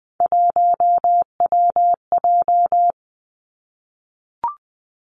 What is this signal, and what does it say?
Morse code, heard as a keyed tone, about 20 wpm, 700 Hz. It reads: 1WJ
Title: Three Challenging Letter-Number Combinations 20wpm